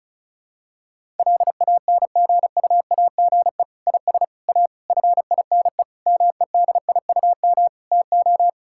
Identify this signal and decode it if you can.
Morse code, heard as a keyed tone, about 34 wpm, 700 Hz. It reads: LANGUAGE IS A FINE MEDIUM TO
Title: Sentences from Top 1000 Words - Review Part 6 - 35wpm